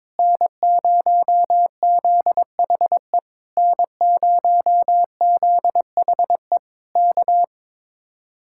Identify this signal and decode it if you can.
Morse code, heard as a keyed tone, about 22 wpm, 700 Hz. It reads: N0ZHE N0ZHE K
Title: Speed-Racing Common QSO Phrases - 22wpm